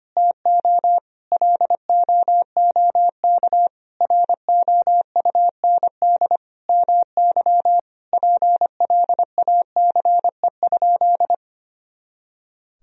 Morse code, heard as a keyed tone, about 25 wpm, 700 Hz. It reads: TO LOOK ROUND MY PLACE?
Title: Sentences from Top 200 Words and Abbreviations 25wpm